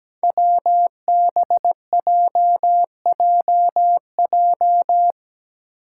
Morse code, heard as a keyed tone, about 17 wpm, 700 Hz. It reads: WBJJJ
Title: Five Challenging Letter-Number Combinations 17wpm